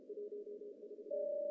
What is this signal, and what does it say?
Morse code, heard as a keyed tone, about 35 wpm, 410 Hz. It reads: 14
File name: CW / cqww-cw-k5tr